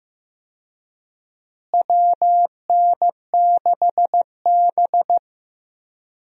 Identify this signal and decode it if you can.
Morse code, heard as a keyed tone, about 15 wpm, 700 Hz. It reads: WN6B
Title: USA Call Signs - 4 Characters - Repeated 2x - 15wpm